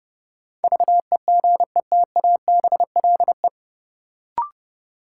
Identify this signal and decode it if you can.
Morse code, heard as a keyed tone, about 30 wpm, 700 Hz. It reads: VEGETABLE
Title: Sentences from Top 1300 Words - New Word Drill Alt - 30wpm